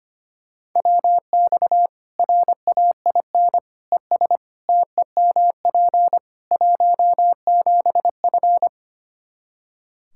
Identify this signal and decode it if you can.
Morse code, heard as a keyed tone, about 25 wpm, 700 Hz. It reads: WX RAIN ES TEMP 17F
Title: Practice QSO - Exchange 2 - Info - 25wpm